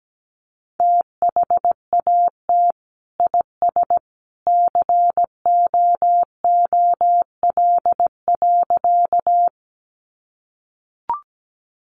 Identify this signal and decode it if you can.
Morse code, heard as a keyed tone, about 17 wpm, 700 Hz. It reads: THAT IS COOL.
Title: Sentences from Top 1200 Words 17wpm